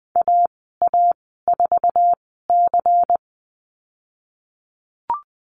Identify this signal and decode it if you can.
Morse code, heard as a keyed tone, about 20 wpm, spaced as for 15 wpm, 700 Hz. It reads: AA4C
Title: 147 Lesson 24g - Letter y - Call Signs 20wpm